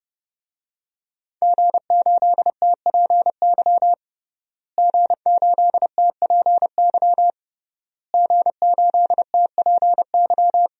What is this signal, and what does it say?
Morse code, heard as a keyed tone, about 30 wpm, 700 Hz. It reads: G8TPY G8TPY G8TPY
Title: DXCC-100 Call Signs - 5 Characters - Repeated 3x - 30wpm